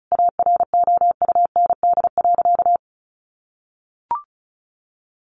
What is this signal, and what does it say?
Morse code, heard as a keyed tone, about 34 wpm, 700 Hz. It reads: AROUND.
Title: Easy Phrases 35wpm